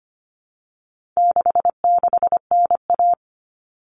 Morse code, heard as a keyed tone, about 25 wpm, 700 Hz. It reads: 66NA
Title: All Cut Numbers - 4-Digit 25wpm